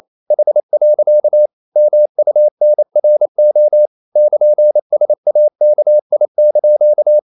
Morse code, heard as a keyed tone, about 28 wpm, 600 Hz. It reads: H. MUNRO <KN>SAKI)
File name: CW / Practice / EnglishQuotations0228